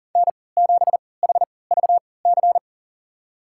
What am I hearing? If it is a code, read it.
Morse code, heard as a keyed tone, about 40 wpm, spaced as for 23 wpm, 690 Hz. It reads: N7HVC
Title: USA Call Signs - 3x Character Spacing - 40wpm